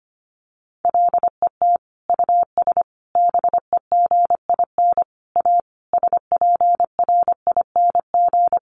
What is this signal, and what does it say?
Morse code, heard as a keyed tone, about 25 wpm, 700 Hz. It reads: LET US BEGIN A SPRING
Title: Sentences from Top 1400 Words 25wpm